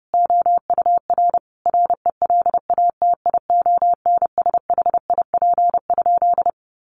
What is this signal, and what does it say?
Morse code, heard as a keyed tone, about 30 wpm, 700 Hz. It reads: OUR RELATIONSHIP?
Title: Sentences from Top 1100 Words 30wpm